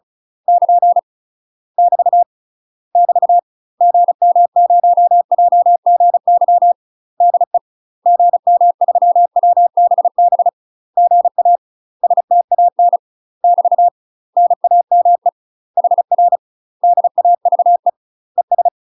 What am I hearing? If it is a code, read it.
Morse code, heard as a keyed tone, about 36 wpm, 700 Hz. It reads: <KN> <BT> <BT> GM0JGY DE GM3WBB GA STAN <BT> NAME HR DAVE ES
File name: UK MASTER QSOs TRAINING / 35WPM / UK MASTER QSOs 0035